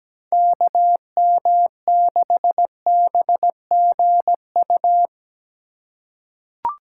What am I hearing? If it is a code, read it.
Morse code, heard as a keyed tone, about 17 wpm, 700 Hz. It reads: KM6BGU
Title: USA Call Signs - 6 Characters - 2x Character Spacing - 17wpm